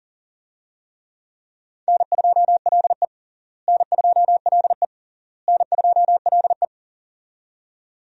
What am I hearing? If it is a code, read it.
Morse code, heard as a keyed tone, about 40 wpm, 700 Hz. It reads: N2LE N2LE N2LE
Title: USA Call Signs - 4 Characters - Repeated 3x - 40wpm